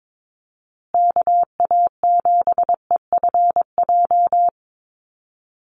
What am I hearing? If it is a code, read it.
Morse code, heard as a keyed tone, about 22 wpm, 700 Hz. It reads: KA7EFJ
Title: USA Call Signs Repeated 2x 22wpm